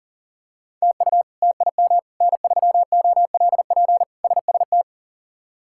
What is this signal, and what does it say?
Morse code, heard as a keyed tone, about 40 wpm, 700 Hz. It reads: TU TIM N3OLP SST
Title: K1USN SST QSO Practice - 40wpm